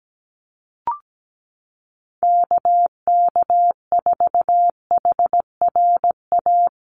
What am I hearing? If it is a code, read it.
Morse code, heard as a keyed tone, about 17 wpm, 700 Hz. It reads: KK4HRA
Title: USA Call Signs Repeated 2x 17wpm